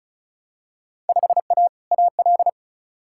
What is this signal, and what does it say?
Morse code, heard as a keyed tone, about 36 wpm, 700 Hz. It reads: 5A AL
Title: Field Day Exchanges Encore - 35wpm